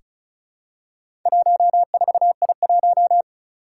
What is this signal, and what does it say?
Morse code, heard as a keyed tone, about 36 wpm, 700 Hz. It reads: 14I1
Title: Four Challenging Letter-Number Combinations 35wpm